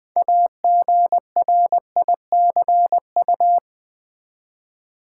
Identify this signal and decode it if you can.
Morse code, heard as a keyed tone, about 20 wpm, 700 Hz. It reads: AGRICU
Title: Build A Word - 6 Letters At A Time - 12-Letter Words - 20wpm